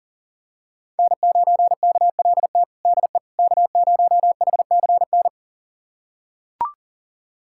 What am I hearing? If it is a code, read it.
Morse code, heard as a keyed tone, about 40 wpm, 700 Hz. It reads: N9KLT DE K0HCN
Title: USA Callsign DE Callsign - 5 Characters - 40wpm